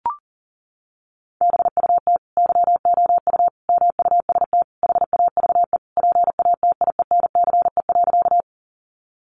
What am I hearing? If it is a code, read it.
Morse code, heard as a keyed tone, about 40 wpm, 700 Hz. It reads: BUT YOU MUST HAVE PATIENCE.
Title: Sentences from Top 1300 Words 40wpm